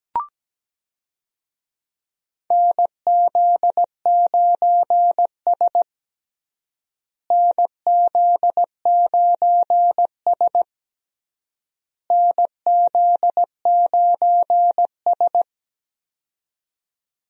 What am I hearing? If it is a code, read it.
Morse code, heard as a keyed tone, about 17 wpm, 700 Hz. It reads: NZ9S NZ9S NZ9S
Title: USA Call Signs - 4 Characters - Repeated 3x - 17wpm